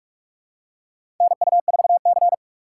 Morse code, heard as a keyed tone, about 46 wpm, 700 Hz. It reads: NU4C
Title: USA Call Signs - 4x Character Spacing - 45wpm